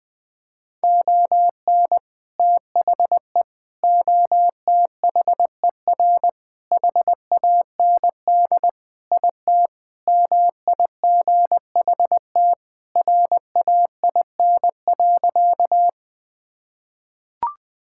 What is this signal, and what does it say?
Morse code, heard as a keyed tone, about 20 wpm, 700 Hz. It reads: ON THE OTHER HAND IT MIGHT RAIN.
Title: Sentences with 3-Word Phrases - Spoken Beginning - 20wpm